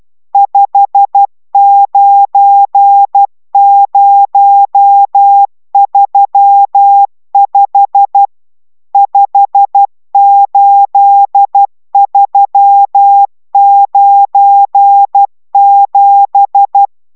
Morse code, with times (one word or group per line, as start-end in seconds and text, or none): 0.3-8.3 59035
8.9-16.9 58397